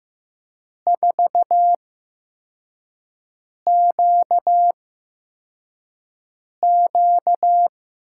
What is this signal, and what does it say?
Morse code, heard as a keyed tone, about 15 wpm, spaced as for 4 wpm, 700 Hz. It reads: 4QQ
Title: USA Call Signs - 6 Characters - 8x Character Spacing - 15wpm